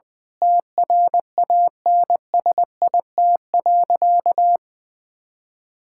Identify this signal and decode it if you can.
Morse code, heard as a keyed tone, about 20 wpm, 700 Hz. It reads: TRANSIT.
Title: Difficult Sentences with Hints - 20wpm